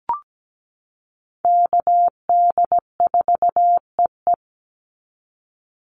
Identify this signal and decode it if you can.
Morse code, heard as a keyed tone, about 17 wpm, 700 Hz. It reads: KD4EE
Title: Call Signs - 17wpm